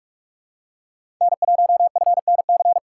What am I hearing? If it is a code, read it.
Morse code, heard as a keyed tone, about 44 wpm, 700 Hz. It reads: N1FNC
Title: USA Call Signs - 5 Characters - 4x Character Spacing - 45wpm